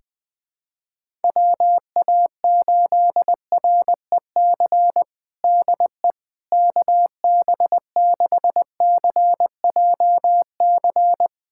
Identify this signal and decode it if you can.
Morse code, heard as a keyed tone, about 20 wpm, 700 Hz. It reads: WA8REC DE KB6CJC
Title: USA Callsign DE Callsign - 6 Characters - 20wpm